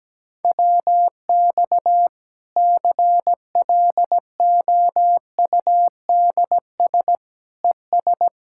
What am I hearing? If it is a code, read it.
Morse code, heard as a keyed tone, about 17 wpm, 700 Hz. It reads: WX CLOUDS ES
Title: Practice QSO - Exchange 2 - Info - 17wpm